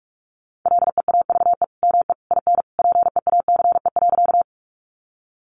Morse code, heard as a keyed tone, about 44 wpm, 700 Hz. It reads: LEAVE ME IN PEACE.
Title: Sentences from Top 1500 Words 45wpm